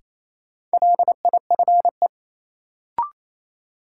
Morse code, heard as a keyed tone, about 28 wpm, 700 Hz. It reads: LIFE
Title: Sentences from Top 300 Words - New Word Drill - 28wpm